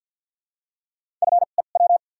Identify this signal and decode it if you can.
Morse code, heard as a keyed tone, about 53 wpm, spaced as for 37 wpm, 700 Hz. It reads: FEW
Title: 2x Character Spacing - Top 300 Words - 50wpm